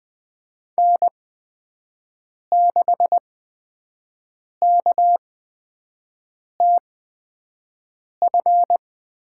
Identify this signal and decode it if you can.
Morse code, heard as a keyed tone, about 20 wpm, spaced as for 5 wpm, 700 Hz. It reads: N6KTF
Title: USA Call Signs - 5 Characters - 8x Character Spacing - 20wpm